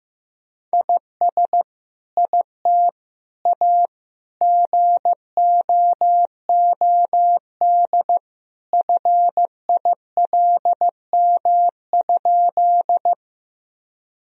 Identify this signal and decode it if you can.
Morse code, heard as a keyed tone, about 15 wpm, 700 Hz. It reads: IS IT A GOOD FILM?
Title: Sentences from Top 1000 Words - Review Part 3 - 15wpm